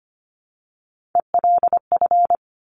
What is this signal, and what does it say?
Morse code, heard as a keyed tone, about 25 wpm, 700 Hz. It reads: ELF
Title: Top Three Letter Combinations in Words 25wpm